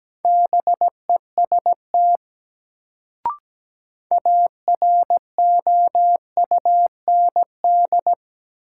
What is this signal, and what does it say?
Morse code, heard as a keyed tone, about 17 wpm, 700 Hz. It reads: BEST AROUND
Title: Sets of 2 Words - New Word Drill - 17wpm